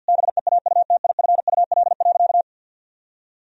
Morse code, heard as a keyed tone, about 48 wpm, 700 Hz. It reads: BEAUTIFUL.
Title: Sentences from Top 1000 Words - Review Part 3 - 50wpm